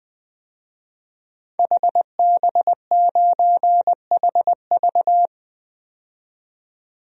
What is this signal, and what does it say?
Morse code, heard as a keyed tone, about 21 wpm, 700 Hz. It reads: HB9HV
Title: DXCC-100 Call Signs - 5 Characters - 20wpm